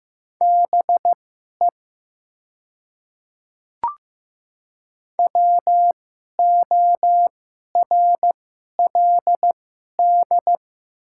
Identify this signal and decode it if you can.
Morse code, heard as a keyed tone, about 15 wpm, spaced as for 11 wpm, 700 Hz. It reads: BE WORLD
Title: 2x Character Spacing - Top 500 Words - 15wpm